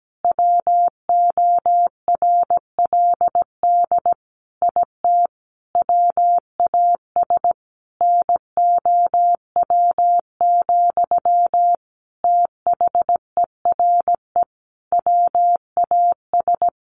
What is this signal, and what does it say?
Morse code, heard as a keyed tone, about 17 wpm, 700 Hz. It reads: WORLD IT WAS NOW, THERE WAS
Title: A Sound Of Thunder 17wpm